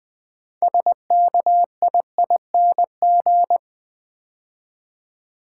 Morse code, heard as a keyed tone, about 20 wpm, 700 Hz. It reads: SKIING
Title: 6 Letter Words - First 3 Letters Given - 20wpm